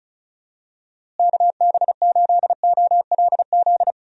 Morse code, heard as a keyed tone, about 34 wpm, 700 Hz. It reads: KB8OLZ
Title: USA Call Signs - 6 Characters - Repeated 3x - 35wpm